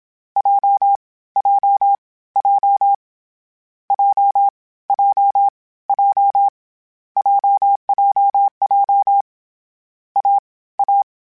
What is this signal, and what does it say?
Morse code, heard as a keyed tone, about 27 wpm, 800 Hz. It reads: J J J J J J JJJ A A
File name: Difficult Combos / Difficult Combos-mp3s / AWJ1 @25